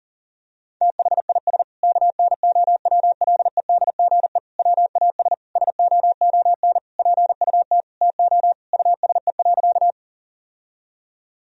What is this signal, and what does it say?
Morse code, heard as a keyed tone, about 37 wpm, 700 Hz. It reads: THIS KNOWLEDGE WAS SOON PUT TO USE.
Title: Sentences from Top 1000 Words - Review Part 3 - 40wpm